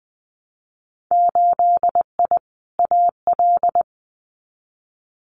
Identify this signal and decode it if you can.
Morse code, heard as a keyed tone, about 19 wpm, 700 Hz. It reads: 8I AL
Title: Winter Field Day Exchanges Encore 20wpm